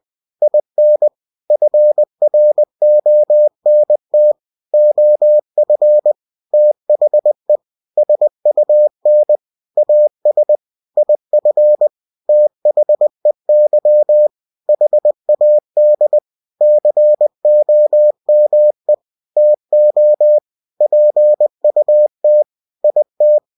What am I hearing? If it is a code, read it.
Morse code, heard as a keyed tone, about 20 wpm, 600 Hz. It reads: IN FRONT OF THE SUN AS IF THEY HAD COME TO PUT IT
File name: cw-books / winnie-the-pooh / winnie-20wpm-page-0028